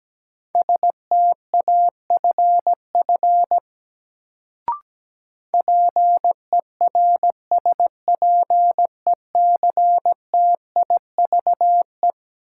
Morse code, heard as a keyed tone, about 17 wpm, 700 Hz. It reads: STAFF PERSPECTIVE
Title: Sentences from Top 1200 Words - New Word Drill Alt - 17wpm